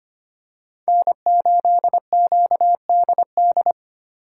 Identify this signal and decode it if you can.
Morse code, heard as a keyed tone, about 25 wpm, 700 Hz. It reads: N8QDD
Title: USA Call Signs - 5 Characters - 3x Character Spacing - 25wpm